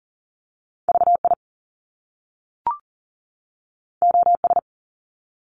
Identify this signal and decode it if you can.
Morse code, heard as a keyed tone, about 40 wpm, 700 Hz. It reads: VI OS
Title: Top Two Letter Combinations in Words 40wpm